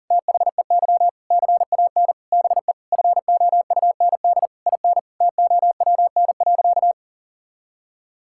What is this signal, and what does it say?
Morse code, heard as a keyed tone, about 40 wpm, 700 Hz. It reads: THEY CAN BE FOUND IN TOWN.
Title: Sentences from Top 1000 Words - Review Part 6 - 40wpm